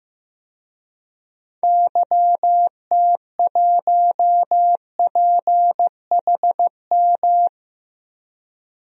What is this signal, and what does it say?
Morse code, heard as a keyed tone, about 15 wpm, 700 Hz. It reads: YT1PHM
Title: DXCC-100 Call Signs - 6 Characters - 15wpm